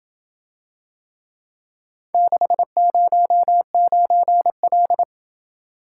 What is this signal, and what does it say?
Morse code, heard as a keyed tone, about 27 wpm, 700 Hz. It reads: 609L